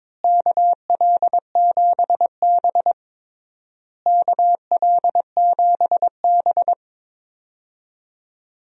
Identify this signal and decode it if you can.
Morse code, heard as a keyed tone, about 22 wpm, 700 Hz. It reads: KL7B KL7B
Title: DXCC-100 Call Signs - 4 Characters - Repeated 3x - 22wpm